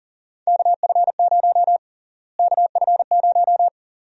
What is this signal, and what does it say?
Morse code, heard as a keyed tone, about 40 wpm, 700 Hz. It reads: KF0 KF0
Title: USA Call Sign Prefixes - Repeated 2x - 40wpm